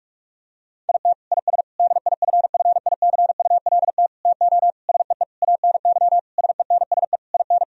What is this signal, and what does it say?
Morse code, heard as a keyed tone, about 46 wpm, 700 Hz. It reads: IT IS DIFFICULT TO SEE ANY SENSE IN
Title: Sentences from Top 1000 Words - Review Part 4 - 45wpm